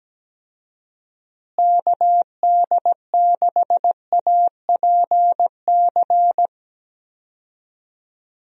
Morse code, heard as a keyed tone, about 17 wpm, 700 Hz. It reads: KD6APC
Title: USA Call Signs - 6 Characters - Repeated 2x - 17wpm